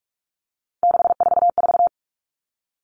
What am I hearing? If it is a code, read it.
Morse code, heard as a keyed tone, about 44 wpm, 700 Hz. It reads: 644
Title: All Cut Numbers - 3-Digit 45wpm